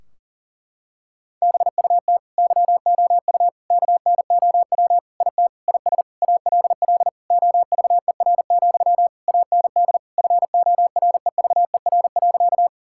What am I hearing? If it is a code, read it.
Morse code, heard as a keyed tone, about 40 wpm, 700 Hz. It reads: BUT YOU KNOW IT IS ALL OVER, AND FOREVER.
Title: Sentences from Top 1600 Words 40wpm